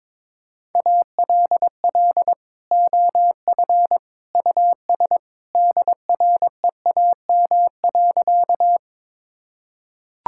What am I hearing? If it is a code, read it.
Morse code, heard as a keyed tone, about 22 wpm, 700 Hz. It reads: ALL OF US DREAM.
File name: practice / 22wpm / Sentences with 2-Word Phrases - 22wpm